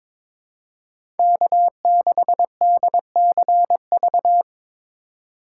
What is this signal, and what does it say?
Morse code, heard as a keyed tone, about 22 wpm, 700 Hz. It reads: K6DCV
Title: USA Call Signs - 4x Character Spacing - 22wpm